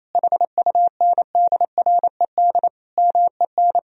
Morse code, heard as a keyed tone, about 27 wpm, 700 Hz. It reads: HUNDRED MEN
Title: Sentences from Top 1000 Words - Review Part 2 - 28wpm